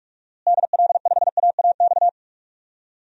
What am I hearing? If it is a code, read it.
Morse code, heard as a keyed tone, about 47 wpm, 700 Hz. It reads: DL5AAX
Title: DXCC-100 Call Signs - 6 Characters - Repeated 2x - 45wpm